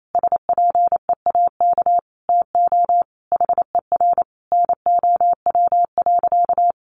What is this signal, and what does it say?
Morse code, heard as a keyed tone, about 28 wpm, 700 Hz. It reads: SPEAK TO HER NOW.
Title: Sentences from Top 1400 Words 28wpm